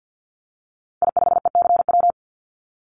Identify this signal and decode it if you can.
Morse code, heard as a keyed tone, about 50 wpm, 700 Hz. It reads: I5ECW
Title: Call Signs - 50wpm